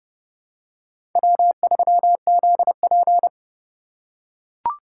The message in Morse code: W3ZP

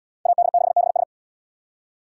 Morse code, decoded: AD6BS

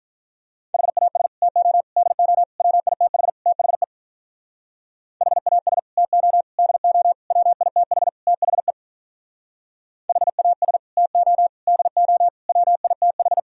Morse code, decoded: HAS TO DO WITH THE HAS TO DO WITH THE HAS TO DO WITH

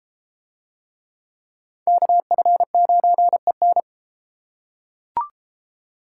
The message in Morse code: KF9EN

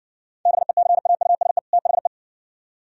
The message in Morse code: BECAUSE THE

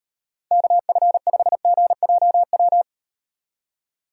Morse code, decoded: KF5GJW

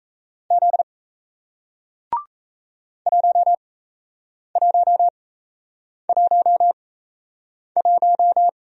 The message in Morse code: Z 1111

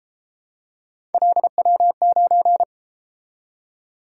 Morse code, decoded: LW9

